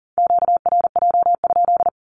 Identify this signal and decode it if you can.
Morse code, heard as a keyed tone, about 40 wpm, 700 Hz. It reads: QRJ?
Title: Q-Codes Repeated 3x 40wpm